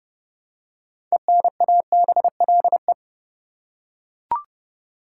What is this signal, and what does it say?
Morse code, heard as a keyed tone, about 31 wpm, 700 Hz. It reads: ENABLE
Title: Six Letter Words - 30wpm